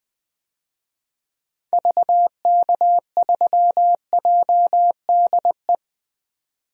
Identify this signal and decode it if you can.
Morse code, heard as a keyed tone, about 20 wpm, 700 Hz. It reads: VK3JDE
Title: DXCC-100 Call Signs - 6 Characters - 20wpm